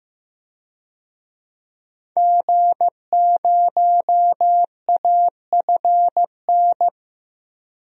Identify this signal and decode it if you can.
Morse code, heard as a keyed tone, about 15 wpm, 700 Hz. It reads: G0AFN
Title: DXCC-100 Call Signs - 5 Characters - 15wpm